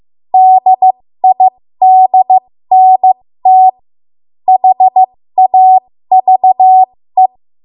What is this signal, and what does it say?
Morse code, heard as a keyed tone, about 15 wpm, spaced as for 13 wpm, 750 Hz. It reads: DIDNT HAVE
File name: Morse / 13WPM